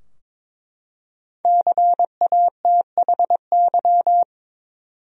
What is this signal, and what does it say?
Morse code, heard as a keyed tone, about 22 wpm, 700 Hz. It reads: CATHY
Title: USA Common Names - Max 5 Letters - 22wpm